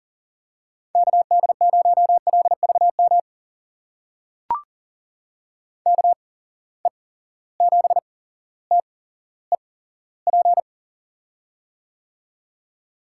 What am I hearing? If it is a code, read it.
Morse code, heard as a keyed tone, about 40 wpm, spaced as for 17 wpm, 700 Hz. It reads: KD0LVM KE7TEP
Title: USA Call Signs - 6 Characters - 8x Character Spacing - 40wpm